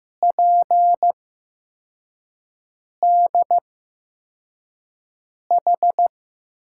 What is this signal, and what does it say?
Morse code, heard as a keyed tone, about 15 wpm, spaced as for 4 wpm, 700 Hz. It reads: PDH
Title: USA Call Signs - 8x Character Spacing - 15wpm